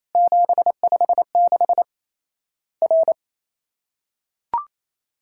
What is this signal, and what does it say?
Morse code, heard as a keyed tone, about 28 wpm, 700 Hz. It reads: 756 R
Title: Worked All Europe DX Contest - QTC Exchange - 28wpm